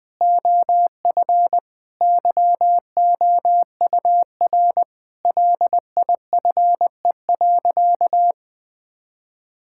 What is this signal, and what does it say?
Morse code, heard as a keyed tone, about 19 wpm, 700 Hz. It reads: OF YOUR LIFE.
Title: Sentences with 5-Word Phrases - 20wpm